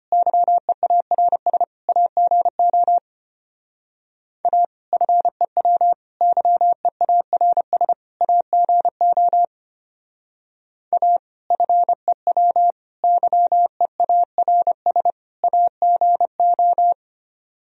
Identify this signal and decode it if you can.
Morse code, heard as a keyed tone, about 31 wpm, 700 Hz. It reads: YEARS AGO A FEW YEARS AGO A FEW YEARS AGO